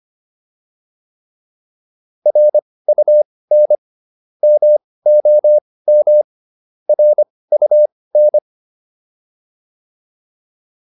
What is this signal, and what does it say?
Morse code, heard as a keyed tone, about 25 wpm, spaced as for 18 wpm, 600 Hz. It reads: RUN MOM RUN
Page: intermediate-practice-files